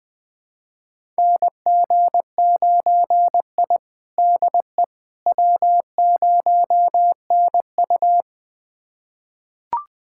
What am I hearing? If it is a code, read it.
Morse code, heard as a keyed tone, about 21 wpm, 700 Hz. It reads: NG9I DE W0NU
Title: USA Callsign DE Callsign - 4 Characters - 20wpm